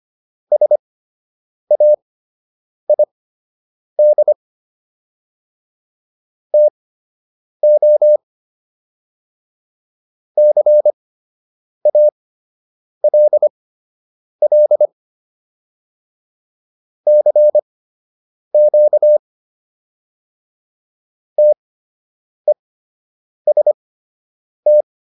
Morse code, heard as a keyed tone, about 25 wpm, spaced as for 8 wpm, 600 Hz. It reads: SAID TO CALL CQ TEST